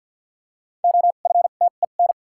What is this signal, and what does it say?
Morse code, heard as a keyed tone, about 49 wpm, spaced as for 36 wpm, 700 Hz. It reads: OFTEN
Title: 2x Character Spacing - Top 300 Words - 50wpm